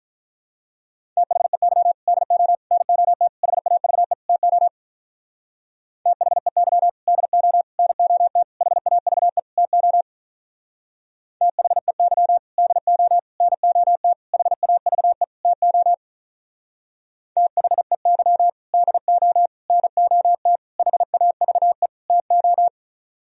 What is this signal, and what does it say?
Morse code, heard as a keyed tone, about 44 wpm, 700 Hz. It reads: THEY DO NOT HAVE TO THEY DO NOT HAVE TO THEY DO NOT HAVE TO THEY DO NOT HAVE TO